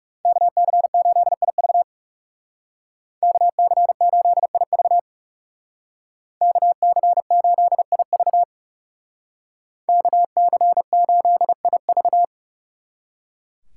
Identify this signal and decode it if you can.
Morse code, heard as a keyed tone, about 35 wpm, 700 Hz. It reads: KC8IV KC8IV KC8IV KC8IV
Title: Speed Racing - USA 2x2 Call Signs 30wpm